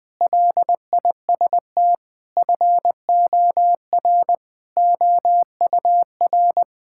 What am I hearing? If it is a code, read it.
Morse code, heard as a keyed tone, about 19 wpm, 700 Hz. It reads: LIST FOR OUR
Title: Sentences with 6-Word Phrases - 20wpm